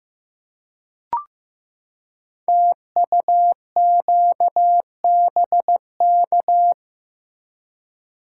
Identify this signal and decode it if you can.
Morse code, heard as a keyed tone, about 15 wpm, 700 Hz. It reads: TUQBK
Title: Code Groups - 5 Letters - 15wpm